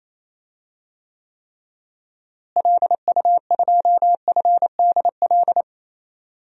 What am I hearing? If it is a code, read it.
Morse code, heard as a keyed tone, about 28 wpm, 700 Hz. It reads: LU2FDL
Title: DXCC-100 Call Signs - 6 Characters - Repeated 3x - 28wpm